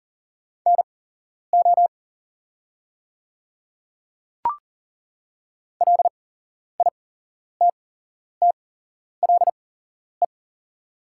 Morse code, heard as a keyed tone, about 38 wpm, spaced as for 11 wpm, 700 Hz. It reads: NO LITTLE
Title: 8x Character Spacing - Top 200 Words - 40wpm